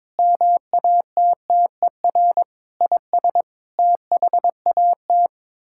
Morse code, heard as a keyed tone, about 21 wpm, 700 Hz. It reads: MATTER IS THAT
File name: practice / 22wpm / Sets of 7 Words - 22wpm